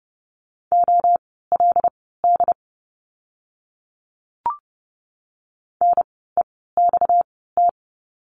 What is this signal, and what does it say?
Morse code, heard as a keyed tone, about 30 wpm, spaced as for 17 wpm, 700 Hz. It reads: OLD NEXT
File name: course / 30:10wpm / 230 Lesson 37f - Letter x - Words 30wpm